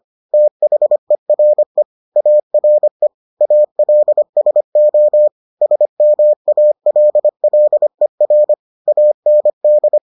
Morse code, heard as a keyed tone, about 25 wpm, 600 Hz. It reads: THERE ARE ALSO SMALLER AND